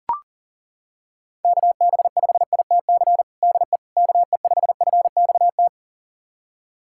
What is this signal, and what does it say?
Morse code, heard as a keyed tone, about 37 wpm, 700 Hz. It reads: KB5ITC DE KE5FXT
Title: USA Callsign DE Callsign - 6 Characters - 40wpm